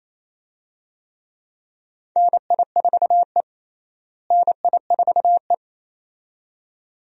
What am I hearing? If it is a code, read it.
Morse code, heard as a keyed tone, about 27 wpm, 700 Hz. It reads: NI4E NI4E
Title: USA Call Signs - 4 Characters - Repeated 2x - 28wpm